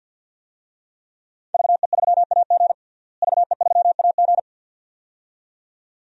Morse code, heard as a keyed tone, about 51 wpm, 700 Hz. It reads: VE3AG VE3AG
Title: DXCC-100 Call Signs - 5 Characters - Repeated 2x - 50wpm